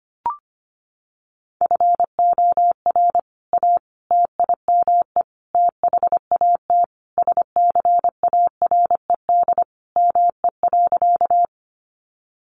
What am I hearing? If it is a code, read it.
Morse code, heard as a keyed tone, about 25 wpm, 700 Hz. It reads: FOR A TIME THAT SCARED ME.
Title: Sentences from Top 700 Words 25wpm